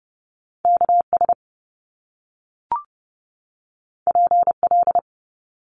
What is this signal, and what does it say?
Morse code, heard as a keyed tone, about 30 wpm, 700 Hz. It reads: KS PL
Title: Top Two Letter Combinations in Words 30wpm